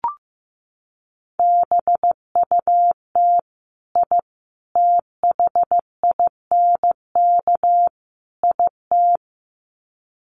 Morse code, heard as a keyed tone, about 15 wpm, 700 Hz. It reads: BUT I THINK IT
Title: Sets of 4 Words 15wpm